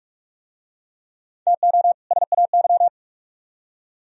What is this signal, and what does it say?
Morse code, heard as a keyed tone, about 40 wpm, 700 Hz. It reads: TO SAY